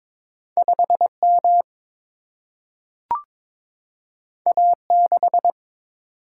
Morse code, heard as a keyed tone, about 22 wpm, 700 Hz. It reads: 5M A6
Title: Single-Letter Single-Number 22wpm